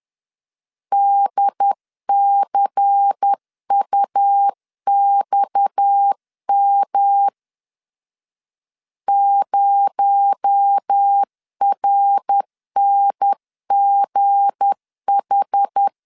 Morse code, heard as a keyed tone, about 10 wpm, 800 Hz. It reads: DCUXM 0RNGH
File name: LEZIONI / 7 GF9Z / 7 G 10wpm 2501